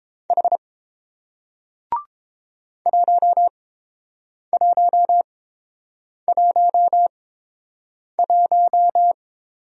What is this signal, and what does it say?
Morse code, heard as a keyed tone, about 31 wpm, spaced as for 7 wpm, 700 Hz. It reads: H 1111